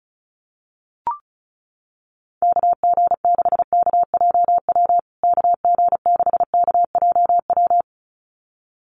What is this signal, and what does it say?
Morse code, heard as a keyed tone, about 34 wpm, 700 Hz. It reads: KG6KJW KG6KJW
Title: USA Call Signs Repeated 2x 35wpm